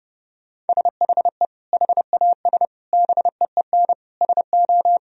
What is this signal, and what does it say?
Morse code, heard as a keyed tone, about 32 wpm, 700 Hz. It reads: SHE HAS BEEN SO
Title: Sentences from Top 1000 Words - Review Part 2 - 30wpm